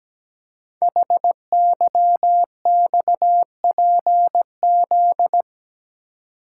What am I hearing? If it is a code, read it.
Morse code, heard as a keyed tone, about 17 wpm, 700 Hz. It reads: HYXPZ
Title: Code Groups - 5 Letters - 17wpm